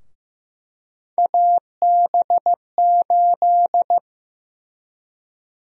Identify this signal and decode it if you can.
Morse code, heard as a keyed tone, about 15 wpm, 700 Hz. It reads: AB8
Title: USA Call Sign Prefixes - 15wpm